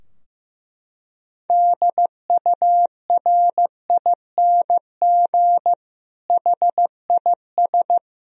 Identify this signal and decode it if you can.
Morse code, heard as a keyed tone, about 15 wpm, 700 Hz. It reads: DURING HIS
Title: Sentences from Top 1000 Words - Review Part 2 - 15wpm